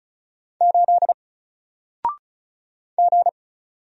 Morse code, heard as a keyed tone, about 36 wpm, 700 Hz. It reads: 8 G
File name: Single-Letter-Number-35wpm